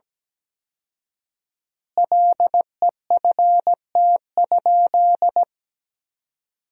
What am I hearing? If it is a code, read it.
Morse code, heard as a keyed tone, about 17 wpm, 700 Hz. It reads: LEFT?
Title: 4x Word-Spacing - Sentences from Top 300 Words - 17wpm